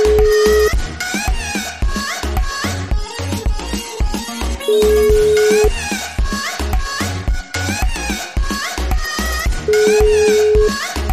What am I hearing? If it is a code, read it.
Ringback tone (Europe, 425 Hz), with noise.